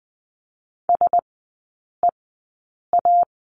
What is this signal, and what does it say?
Morse code, heard as a keyed tone, about 20 wpm, spaced as for 8 wpm, 700 Hz. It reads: SEA
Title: 093 Lesson 16e - Letter c - Three Characters 20wpm